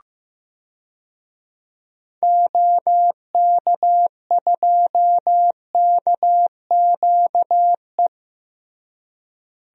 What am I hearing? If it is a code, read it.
Morse code, heard as a keyed tone, about 15 wpm, 700 Hz. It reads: OK2KQE
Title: DXCC-100 Call Signs - 6 Characters - Repeated 2x - 15wpm